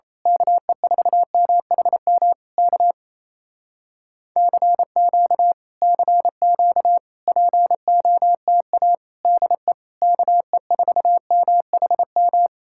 Morse code, heard as a keyed tone, about 27 wpm, 700 Hz. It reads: KE4MHM K CQ CQ POTA DE KE4MHM
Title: Speed-Racing Common QSO Phrases Encore - 28wpm